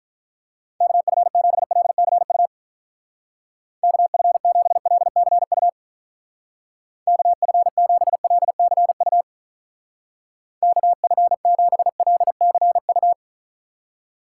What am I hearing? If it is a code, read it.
Morse code, heard as a keyed tone, about 47 wpm, 700 Hz. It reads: KF7LCU KF7LCU KF7LCU KF7LCU